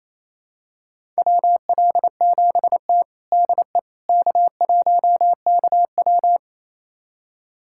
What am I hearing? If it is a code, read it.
Morse code, heard as a keyed tone, about 28 wpm, 700 Hz. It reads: WL7T DE K1KW
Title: USA Callsign DE Callsign - 4 Characters - 28wpm